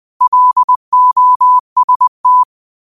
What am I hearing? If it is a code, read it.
Morse code, heard as a keyed tone, about 20 wpm, 1000 Hz. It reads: LOST
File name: qotdfiles / 20wpm